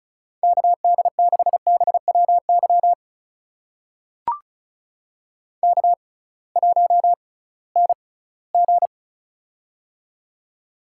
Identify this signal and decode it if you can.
Morse code, heard as a keyed tone, about 36 wpm, spaced as for 20 wpm, 700 Hz. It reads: KD6BWY K1NG
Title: USA Call Signs - 6x Character Spacing - 35wpm